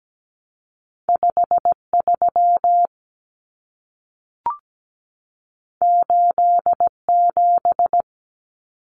Morse code, heard as a keyed tone, about 17 wpm, 700 Hz. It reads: 53 87
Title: Common Cut Numbers - 2-Digit 17wpm